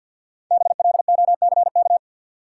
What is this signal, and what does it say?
Morse code, heard as a keyed tone, about 50 wpm, 700 Hz. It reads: BLOCK
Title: Sentences from Top 1600 Words - New Word Drill - 50wpm